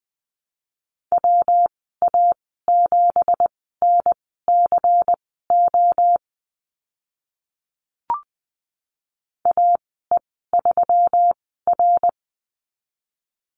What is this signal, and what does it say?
Morse code, heard as a keyed tone, about 20 wpm, spaced as for 14 wpm, 700 Hz. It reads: WA7NCO AE3R
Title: 193 Lesson 31g - Letter v - Call Signs 20wpm